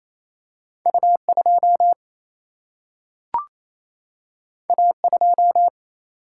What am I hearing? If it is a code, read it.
Morse code, heard as a keyed tone, about 28 wpm, 700 Hz. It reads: U 2 A 2
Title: Single-Letter Single-Number 28wpm